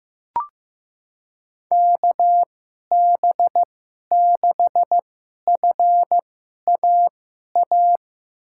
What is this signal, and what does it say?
Morse code, heard as a keyed tone, about 15 wpm, spaced as for 11 wpm, 700 Hz. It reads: KB6FAA
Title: USA Call Signs - 6 Characters - 2x Character Spacing - 15wpm